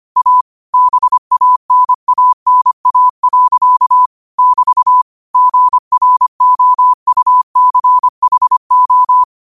Morse code, heard as a keyed tone, about 25 wpm, 1000 Hz. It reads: A BANANA. <BT> GROUCHO